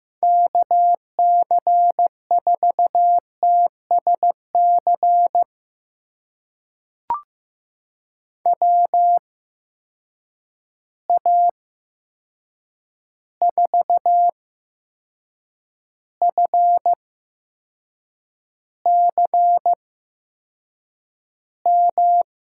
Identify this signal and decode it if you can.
Morse code, heard as a keyed tone, about 15 wpm, spaced as for 6 wpm, 700 Hz. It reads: KC4TSC WA4FCM
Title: USA Call Signs - 6 Characters - 8x Character Spacing - 15wpm